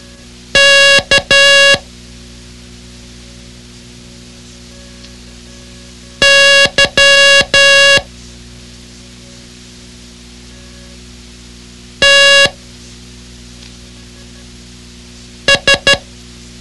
Morse code, heard as a keyed tone, about 12 wpm, 2800 Hz. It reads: K Y T S
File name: ham radio / CodeQuick / Code Quick 06